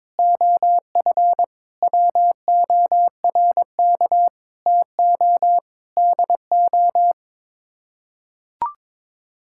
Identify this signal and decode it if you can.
Morse code, heard as a keyed tone, about 22 wpm, 700 Hz. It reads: OF WORK TO DO
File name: Sets of 9 Words - 22wpm